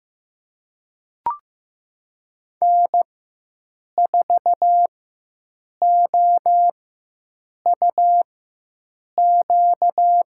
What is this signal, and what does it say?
Morse code, heard as a keyed tone, about 15 wpm, spaced as for 7 wpm, 700 Hz. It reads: N4OUQ
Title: USA Call Signs - 4x Character Spacing - 15wpm